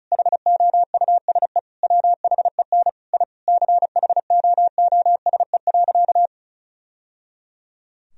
Morse code, heard as a keyed tone, about 34 wpm, 700 Hz. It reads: HOUSE WHEN I CHOOSE.
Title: Sentences from Top 1000 Words - Review Part 4 - 35wpm